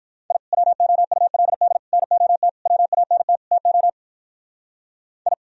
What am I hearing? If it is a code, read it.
Morse code, heard as a keyed tone, about 43 wpm, 700 Hz. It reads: I WOULD NOT WANT TO I
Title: Speed-Racing - Sets of 5 Words 35wpm